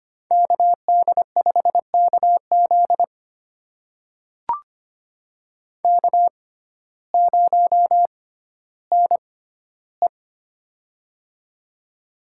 Morse code, heard as a keyed tone, about 25 wpm, spaced as for 14 wpm, 700 Hz. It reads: KD5KZ K0NE
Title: USA Call Signs - 6x Character Spacing - 25wpm